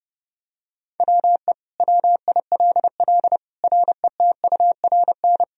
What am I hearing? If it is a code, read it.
Morse code, heard as a keyed tone, about 30 wpm, 700 Hz. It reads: WE WILL RETURN
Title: Sentences from Top 1000 Words - Review Part 3 - 30wpm